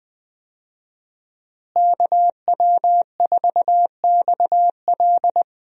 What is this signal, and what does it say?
Morse code, heard as a keyed tone, about 20 wpm, 700 Hz. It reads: KW4XL
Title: USA Call Signs - 5 Characters - Repeated 3x - 20wpm